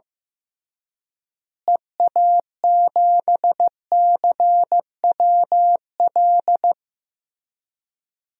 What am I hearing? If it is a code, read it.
Morse code, heard as a keyed tone, about 14 wpm, 700 Hz. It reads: EA7CWL
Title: DXCC-100 Call Signs - 6 Characters - Repeated 3x - 15wpm